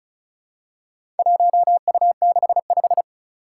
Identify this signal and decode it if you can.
Morse code, heard as a keyed tone, about 36 wpm, 700 Hz. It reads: 1U65
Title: Four Challenging Letter-Number Combinations 35wpm